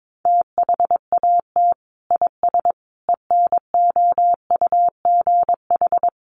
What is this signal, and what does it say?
Morse code, heard as a keyed tone, about 22 wpm, 700 Hz. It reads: THAT IS ENOUGH
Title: Sentences from Top 400 Words 22wpm